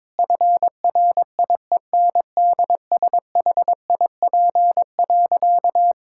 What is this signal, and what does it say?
Morse code, heard as a keyed tone, about 22 wpm, 700 Hz. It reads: FRIENDSHIP.
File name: Difficult Sentences with Hints - 22wpm